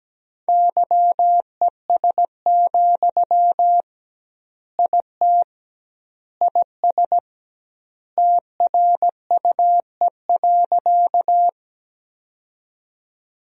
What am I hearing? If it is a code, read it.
Morse code, heard as a keyed tone, about 17 wpm, 700 Hz. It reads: YES, IT IS TRUE.
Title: 2x Word-Spacing - Sentences from Top 500 Words - 17wpm